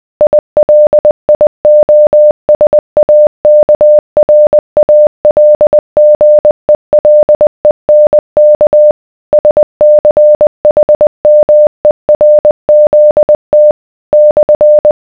Morse code, heard as a keyed tone, about 20 wpm, 600 Hz. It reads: ILIOSAKRALGELENK SCHMERZT /